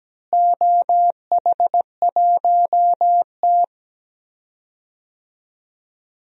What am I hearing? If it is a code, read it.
Morse code, heard as a keyed tone, about 17 wpm, 700 Hz. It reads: OH1T
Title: DXCC-100 Call Signs - 4 Characters - Repeated 3x - 17wpm